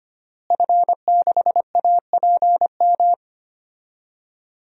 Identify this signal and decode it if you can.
Morse code, heard as a keyed tone, about 25 wpm, 700 Hz. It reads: F6APM
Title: DXCC-100 Call Signs - 5 Characters - Repeated 3x - 25wpm